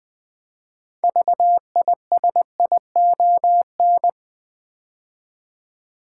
Morse code, heard as a keyed tone, about 20 wpm, 700 Hz. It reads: VISION
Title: 6 Letter Words - First 3 Letters Given - 20wpm